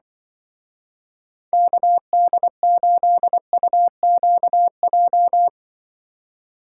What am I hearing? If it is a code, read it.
Morse code, heard as a keyed tone, about 23 wpm, 700 Hz. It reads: KD8UQJ
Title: Speed Racing - USA 2x3 Call Signs 20wpm